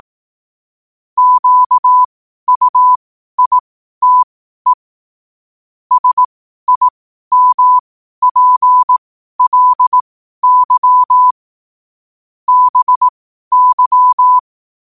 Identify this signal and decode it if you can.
Morse code, heard as a keyed tone, about 18 wpm, spaced as for 12 wpm, 1000 Hz. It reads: QUITE SIMPLY BY
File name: qotdfiles / 12wpm